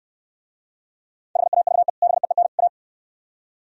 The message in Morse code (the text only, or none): HAVE BEEN A